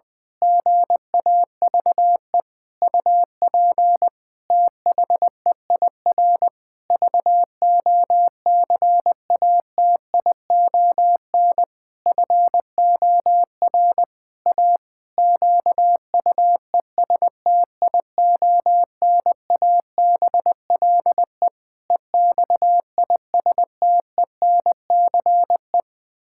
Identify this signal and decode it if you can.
Morse code, heard as a keyed tone, about 21 wpm, 700 Hz. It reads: GAVE UP THEIR VOCATION FOR A QUESTIONABLE EXISTENCE
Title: Pangram Challenge v2 - 20wpm